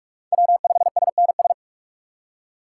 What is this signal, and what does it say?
Morse code, heard as a keyed tone, about 44 wpm, 690 Hz. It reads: W5SNS
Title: USA Call Signs - 5 Characters - 2x Character Spacing - 45wpm